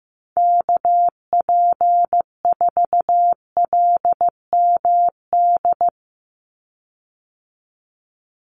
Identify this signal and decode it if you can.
Morse code, heard as a keyed tone, about 15 wpm, 700 Hz. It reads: KP4LMD
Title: USA Call Signs 15wpm